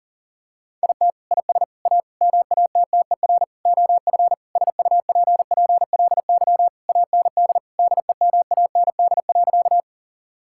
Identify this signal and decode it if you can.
Morse code, heard as a keyed tone, about 40 wpm, 700 Hz. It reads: IT IS A MATTER OF SUPPLY AND DEMAND.
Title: Sentences from Top 1000 Words - Review Part 5 - 40wpm